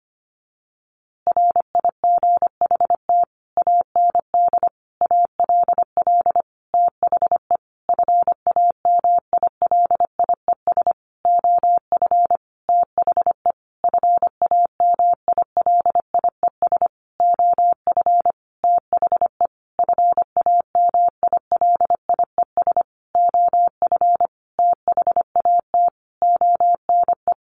Morse code, heard as a keyed tone, about 25 wpm, 700 Hz. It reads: RIGHT AND ALL THE FAMILIES OF THE FAMILIES OF THE FAMILIES OF THAT ONE
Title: A Sound Of Thunder 25wpm